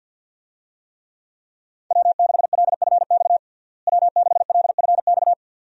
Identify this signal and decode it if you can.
Morse code, heard as a keyed tone, about 49 wpm, 700 Hz. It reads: W6LFX W6LFX
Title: USA Call Signs - 5 Characters - Repeated 2x - 50wpm